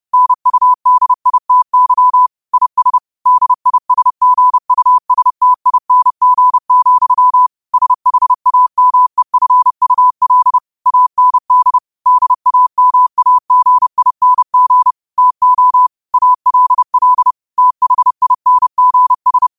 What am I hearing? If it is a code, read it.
Morse code, heard as a keyed tone, about 30 wpm, 1000 Hz. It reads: NUDITY IS DISGUSTING, SHAMEFUL AND DAMAGING TO ALL THINGS
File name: qotdfiles / 30wpm